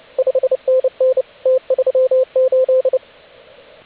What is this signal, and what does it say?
Morse code, heard as a keyed tone, about 29 wpm, 520 Hz, with radio noise. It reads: 5NN T38
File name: AUDIO2019 / HG8DX CW